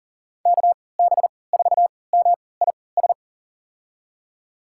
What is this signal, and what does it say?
Morse code, heard as a keyed tone, about 40 wpm, spaced as for 23 wpm, 700 Hz. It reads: KB4MIS
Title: USA Call Signs - 6 Characters - 3x Character Spacing - 40wpm